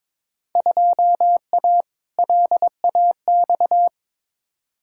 Morse code, heard as a keyed tone, about 22 wpm, 700 Hz. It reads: 2A LAX
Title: Field Day Exchanges - 22wpm